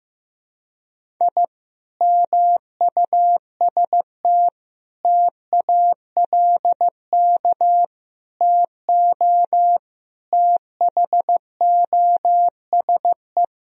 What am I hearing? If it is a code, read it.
Morse code, heard as a keyed tone, about 16 wpm, 700 Hz. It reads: I MUST TALK TO THOSE
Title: Sentences from Top 1000 Words - Review Part 1 - 15wpm